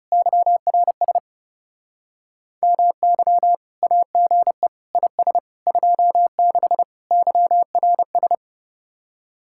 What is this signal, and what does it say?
Morse code, heard as a keyed tone, about 40 wpm, 700 Hz. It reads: YRS MY AGE IS 26 YRS